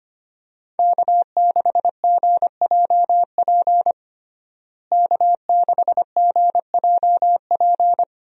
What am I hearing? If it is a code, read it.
Morse code, heard as a keyed tone, about 25 wpm, 700 Hz. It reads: K6GJP K6GJP
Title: USA Call Signs - 5 Characters - Repeated 2x - 25wpm